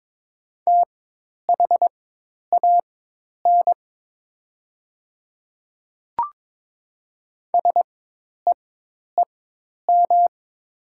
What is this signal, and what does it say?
Morse code, heard as a keyed tone, about 22 wpm, spaced as for 10 wpm, 700 Hz. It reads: THAN SEEM
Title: 4x Character Spacing - Top 500 Words - 22wpm